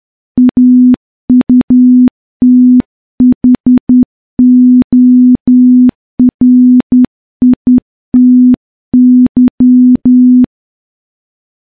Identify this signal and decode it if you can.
Morse code, heard as a keyed tone, about 10 wpm, 250 Hz. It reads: AUTHORITY
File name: morse-training-10WPM